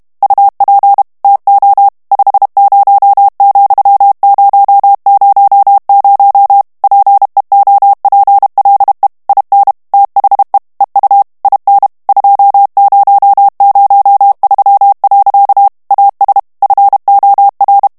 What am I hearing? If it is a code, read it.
Morse code, heard as a keyed tone, about 32 wpm, 800 Hz. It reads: UP TO 50,000 PEOPLE IN THE EU IN 2003. AS FOR
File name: ANGLAIS texte clair 2 vitesse 30 mots minute